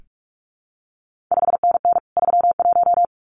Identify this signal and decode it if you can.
Morse code, heard as a keyed tone, about 46 wpm, 700 Hz. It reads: 5NN 31
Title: CQ WW DX Contest Exchange - 45wpm